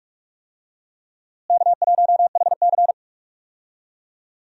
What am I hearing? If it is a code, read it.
Morse code, heard as a keyed tone, about 46 wpm, 700 Hz. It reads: K1HC
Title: USA Call Signs - 4 Characters - Repeated 2x - 45wpm